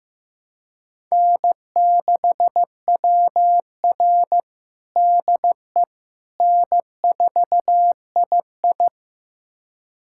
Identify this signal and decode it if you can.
Morse code, heard as a keyed tone, about 15 wpm, 700 Hz. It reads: N6WR DE N4II
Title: USA Callsign DE Callsign - 4 Characters - 15wpm